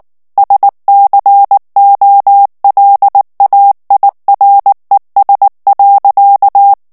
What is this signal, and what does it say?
Morse code, heard as a keyed tone, about 19 wpm, 800 Hz. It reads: SCOLAIRES.
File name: FRANCAIS TEXTE CLAIR 1 VITESSE 18 MOTS MINUTES